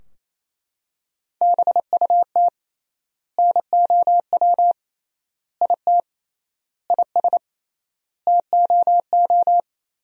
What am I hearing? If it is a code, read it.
Morse code, heard as a keyed tone, about 28 wpm, 700 Hz. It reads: BUT NOW IT IS TOO
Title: 3x Word-Spacing - Sentences from Top 300 Words - 28wpm